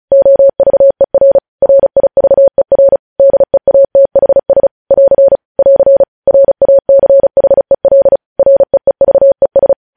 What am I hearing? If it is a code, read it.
Morse code, heard as a keyed tone, about 36 wpm, 550 Hz. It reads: OVER RIVER DEATHS <AR> <AR> RACHEL REEVES
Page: news headlines in morse code at 35 wpm